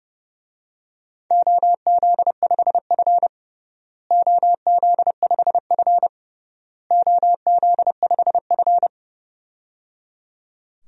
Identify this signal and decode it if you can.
Morse code, heard as a keyed tone, about 30 wpm, 700 Hz. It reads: OZ5F OZ5F OZ5F
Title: DXCC-100 Call Signs - 4 Characters - Repeated 3x - 30wpm